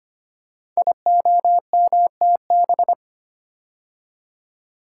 Morse code, heard as a keyed tone, about 25 wpm, 700 Hz. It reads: IOMTB
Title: Code Groups - 5 Letters - 25wpm